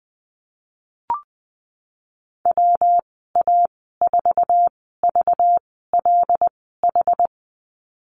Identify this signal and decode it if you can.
Morse code, heard as a keyed tone, about 19 wpm, spaced as for 14 wpm, 700 Hz. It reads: WA4VLH
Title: 193 Lesson 31g - Letter v - Call Signs 20wpm